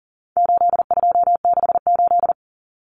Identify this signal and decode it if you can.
Morse code, heard as a keyed tone, about 40 wpm, 700 Hz. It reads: 8268
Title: Numbers - 4-Digit 40wpm